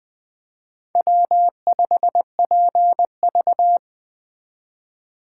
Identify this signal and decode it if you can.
Morse code, heard as a keyed tone, about 24 wpm, 700 Hz. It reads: W5PV
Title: Speed Racing - USA 1x2 Call Signs 20wpm